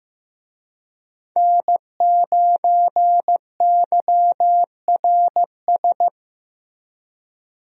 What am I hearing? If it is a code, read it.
Morse code, heard as a keyed tone, about 15 wpm, 700 Hz. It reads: N9YRS
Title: USA Call Signs - 5 Characters - Repeated 3x - 15wpm